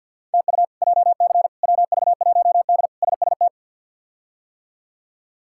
Morse code, heard as a keyed tone, about 47 wpm, 690 Hz. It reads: TU JC WV1D SST
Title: K1USN SST QSO Practice - 50wpm